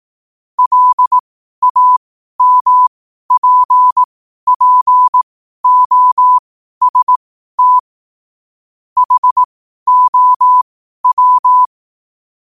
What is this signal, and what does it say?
Morse code, heard as a keyed tone, about 18 wpm, spaced as for 12 wpm, 1000 Hz. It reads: LAMPPOST HOW
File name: qotdfiles / 12wpm